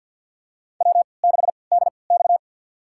Morse code, heard as a keyed tone, about 47 wpm, spaced as for 28 wpm, 700 Hz. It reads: W6DX
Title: USA Call Signs - 4 Characters - 3x Character Spacing - 50wpm